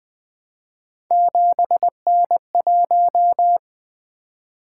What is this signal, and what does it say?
Morse code, heard as a keyed tone, about 20 wpm, 700 Hz. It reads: 7N1
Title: Three Letter-Number Combinations - 20wpm